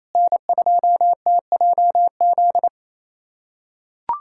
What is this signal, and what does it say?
Morse code, heard as a keyed tone, about 28 wpm, 700 Hz. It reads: N2TJZ
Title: USA Callsign DE Callsign 28wpm